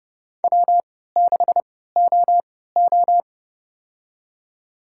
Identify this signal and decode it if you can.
Morse code, heard as a keyed tone, about 32 wpm, spaced as for 17 wpm, 700 Hz. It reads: W6OO
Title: USA Call Signs - 3x Character Spacing - 30wpm